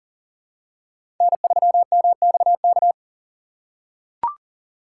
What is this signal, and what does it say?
Morse code, heard as a keyed tone, about 37 wpm, 700 Hz. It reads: N3MXK
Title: USA Call Signs - 5 Characters - 3x Character Spacing - 40wpm